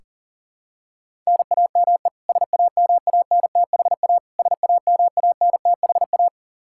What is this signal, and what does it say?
Morse code, heard as a keyed tone, about 40 wpm, 700 Hz. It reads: NAME SAMANTHA SAMANTHA
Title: Practice QSO - Exchange 1 - Beginning - 40wpm